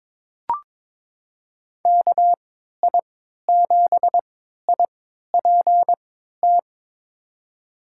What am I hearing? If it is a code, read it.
Morse code, heard as a keyed tone, about 22 wpm, spaced as for 13 wpm, 700 Hz. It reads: KI7IPT
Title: USA Call Signs - 3x Character Spacing - 22wpm